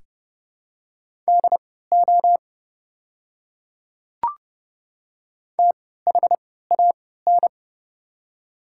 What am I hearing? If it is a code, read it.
Morse code, heard as a keyed tone, about 30 wpm, spaced as for 17 wpm, 700 Hz. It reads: DO THAN
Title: 3x Character Spacing - Top 100 Words - 30wpm